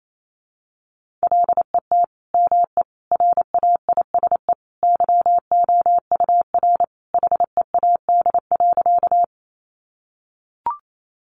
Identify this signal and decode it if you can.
Morse code, heard as a keyed tone, about 28 wpm, 700 Hz. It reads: LET ME RAISE YOUR HEAD.
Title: Sentences from Top 1400 Words 28wpm